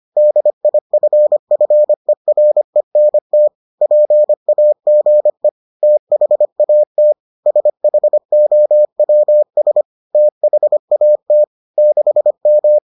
Morse code, heard as a keyed tone, about 25 wpm, 600 Hz. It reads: DIFFERENT PAGE THAT SHOWS THAT 6M